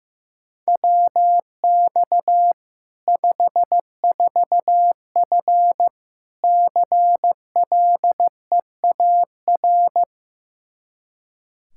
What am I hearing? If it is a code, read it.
Morse code, heard as a keyed tone, about 15 wpm, 700 Hz. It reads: WX 54F CLEAR
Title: K1USN SST QSO Practice 15wpm